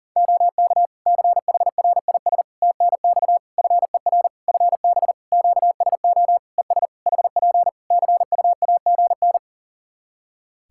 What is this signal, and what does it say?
Morse code, heard as a keyed tone, about 40 wpm, 700 Hz. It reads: OK CHRIS TNX FER FB QSO ES HP CUAGN
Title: Practice QSO - Exchange 4 - Ending - 40wpm